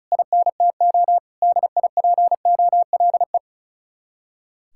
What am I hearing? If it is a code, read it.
Morse code, heard as a keyed tone, about 45 wpm, 700 Hz. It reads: INTO DIPOLE